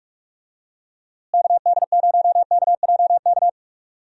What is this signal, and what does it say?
Morse code, heard as a keyed tone, about 43 wpm, 700 Hz. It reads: KD0KJK
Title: USA Call Signs - 6 Characters - Repeated 3x - 45wpm